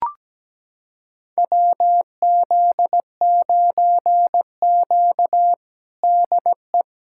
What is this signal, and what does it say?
Morse code, heard as a keyed tone, about 17 wpm, 700 Hz. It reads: WZ9Q DE
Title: USA Callsign DE Callsign 17wpm